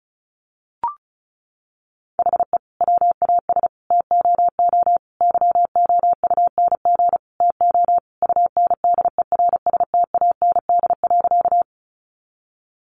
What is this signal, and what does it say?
Morse code, heard as a keyed tone, about 34 wpm, 700 Hz. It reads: HE WAS TOO YOUNG TO UNDERSTAND.
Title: Sentences from Top 600 Words 35wpm